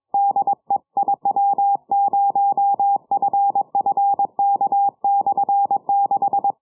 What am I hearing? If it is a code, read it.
Morse code, heard as a keyed tone, about 22 wpm, 800 Hz. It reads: DE IW0FFK/6
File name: iw0ffk / 70cm / 2007 10 07 UR7D IW0FFK 6 980km